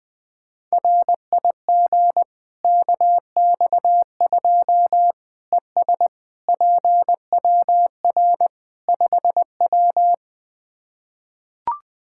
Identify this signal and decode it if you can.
Morse code, heard as a keyed tone, about 20 wpm, 700 Hz. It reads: RIG KX2 ES PWR 5W
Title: K1USN SST QSO Practice 20wpm